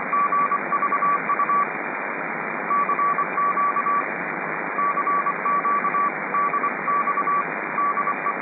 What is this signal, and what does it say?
Morse code, heard as a keyed tone, about 24 wpm, 1150 Hz, with noise. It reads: GVU CQ CQ DX DE